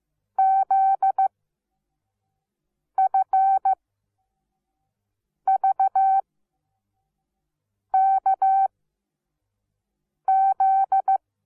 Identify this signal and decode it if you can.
Morse code, heard as a keyed tone, about 15 wpm, spaced as for 5 wpm, 770 Hz. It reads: ZFVKZ
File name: Onlinelehrgang / morsen / DJ4UF Morsen17